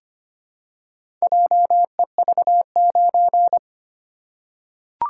Morse code, heard as a keyed tone, about 25 wpm, 700 Hz. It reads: JEV9
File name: practice / 25wpm / Four Letter-Number Combinations - 25wpm